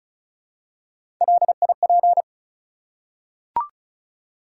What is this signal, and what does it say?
Morse code, heard as a keyed tone, about 34 wpm, 700 Hz. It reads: LIP
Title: Three Letter Words - 35wpm